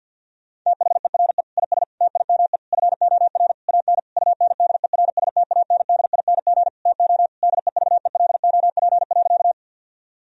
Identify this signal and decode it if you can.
Morse code, heard as a keyed tone, about 50 wpm, 700 Hz. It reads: THERE IS TIME FOR AN UNDERSTANDING TO DEVELOP.
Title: Sentences from Top 1000 Words - Review Part 3 - 50wpm